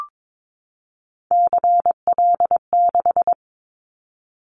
Morse code, heard as a keyed tone, about 22 wpm, 700 Hz. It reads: CL6
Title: Callsign Prefixes for Top 100 DXCC - 22wpm